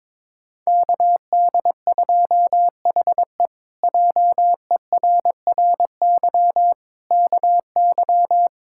Morse code, heard as a keyed tone, about 23 wpm, 700 Hz. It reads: KD2HE JERRY KY
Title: K1USN SST QSO Practice - 22wpm